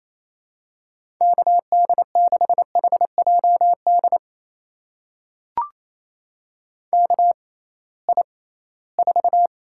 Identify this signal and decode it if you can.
Morse code, heard as a keyed tone, about 28 wpm, spaced as for 18 wpm, 700 Hz. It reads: KD6HJD KI4
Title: USA Call Signs - 6 Characters - 6x Character Spacing - 28wpm